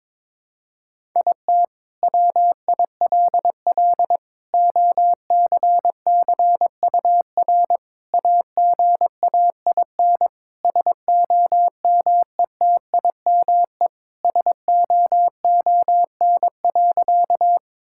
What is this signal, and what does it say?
Morse code, heard as a keyed tone, about 22 wpm, 700 Hz. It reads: IT WILL OCCUR AGAIN SOMETIME SOON.
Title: Sentences from Top 1000 Words - Review Part 6 - 22wpm